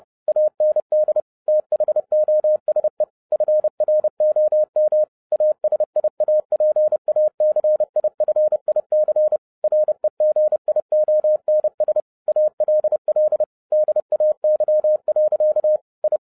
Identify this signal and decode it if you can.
Morse code, heard as a keyed tone, about 30 wpm, 610 Hz. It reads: AND THOSE FROM ASIAPACIFIC REGIONS ALL DAY. I
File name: 2012-01-mp3 / n12-I DIDNT MISS THE SOLAR CYCLE 24-30wpm